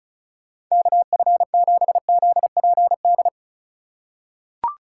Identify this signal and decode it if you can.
Morse code, heard as a keyed tone, about 36 wpm, 700 Hz. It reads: KF7ZPD
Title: USA Call Signs - 4x Character Spacing - 35wpm